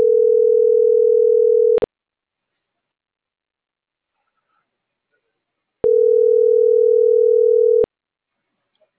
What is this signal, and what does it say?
Ringback tone (North America), 440 + 480 Hz.